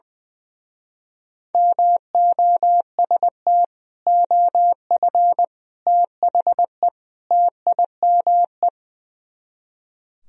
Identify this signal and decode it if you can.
Morse code, heard as a keyed tone, about 21 wpm, 700 Hz. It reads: MOST OF THE TIME
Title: Speed-Racing - Sets of 4 Words 20wpm